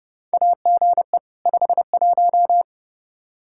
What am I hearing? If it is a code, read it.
Morse code, heard as a keyed tone, about 32 wpm, 700 Hz. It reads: AGE 51
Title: Practice QSO - Exchange 3 - More Info - 30wpm